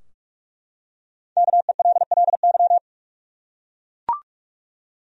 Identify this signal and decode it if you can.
Morse code, heard as a keyed tone, about 44 wpm, 700 Hz. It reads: KELLY
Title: USA Common Names - 45wpm